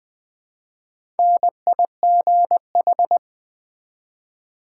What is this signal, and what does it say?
Morse code, heard as a keyed tone, about 20 wpm, 700 Hz. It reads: NIGH
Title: Build A Word - 1 Letter At A Time - 5-Letter Words - 20wpm